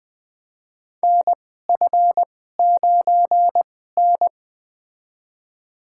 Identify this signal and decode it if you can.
Morse code, heard as a keyed tone, about 20 wpm, spaced as for 14 wpm, 700 Hz. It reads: NF9N
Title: USA Call Signs - 4 Characters - 2x Character Spacing - 20wpm